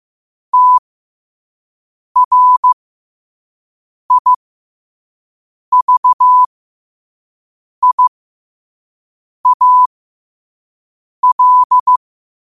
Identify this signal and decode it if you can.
Morse code, heard as a keyed tone, about 15 wpm, spaced as for 5 wpm, 1000 Hz. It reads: TRIVIAL